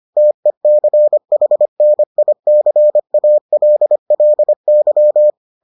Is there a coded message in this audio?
Yes: Morse code from 0.2 to 5.3 s, heard as a keyed tone, about 25 wpm, 600 Hz. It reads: TECHNICALLY